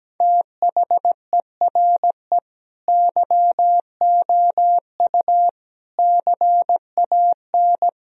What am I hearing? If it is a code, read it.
Morse code, heard as a keyed tone, about 17 wpm, 700 Hz. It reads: THERE YOU CAN
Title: Sentences from Top 1000 Words - Review Part 2 - 17wpm